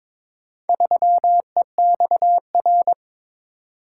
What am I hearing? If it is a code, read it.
Morse code, heard as a keyed tone, about 22 wpm, 700 Hz. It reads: 3EXR
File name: practice / 22wpm / Four Letter-Number Combinations - 22wpm